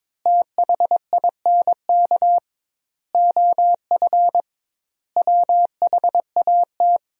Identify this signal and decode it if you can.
Morse code, heard as a keyed tone, about 22 wpm, 700 Hz. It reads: THINK OF WHAT
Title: 2x Word-Spacing - Sentences from Top 400 Words - 22wpm